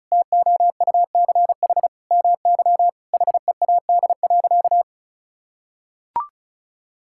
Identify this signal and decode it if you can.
Morse code, heard as a keyed tone, about 34 wpm, 700 Hz. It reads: TOUCH MY HEAD.
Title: Sentences from Top 1600 Words 35wpm